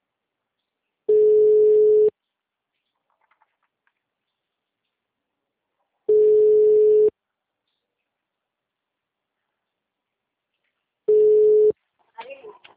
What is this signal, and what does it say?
Ringback tone (Europe, 425 Hz).